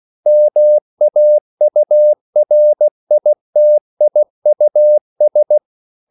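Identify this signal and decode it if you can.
Morse code, heard as a keyed tone, about 16 wpm, 600 Hz. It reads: MAURITIUS
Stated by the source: CW practice of national capitals at 16 words per minute, from ON6ZQ.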